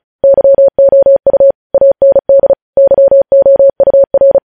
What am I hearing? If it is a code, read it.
Morse code, heard as a keyed tone, about 36 wpm, 550 Hz. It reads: YOU AND YOUR
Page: News Headlines in Morse Code at 35 WPM